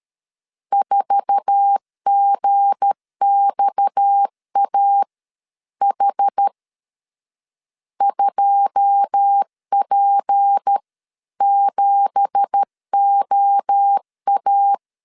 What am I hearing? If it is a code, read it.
Morse code, heard as a keyed tone, about 13 wpm, spaced as for 11 wpm, 800 Hz. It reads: 4GXAH 2P7OA
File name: CW / DET / 1 9 1500C S B 12wpm 3819